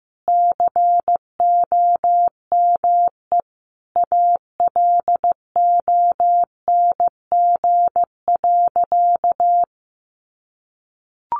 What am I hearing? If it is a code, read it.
Morse code, heard as a keyed tone, about 15 wpm, 700 Hz. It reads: COME ALONG.
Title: Sentences from Top 700 Words 15wpm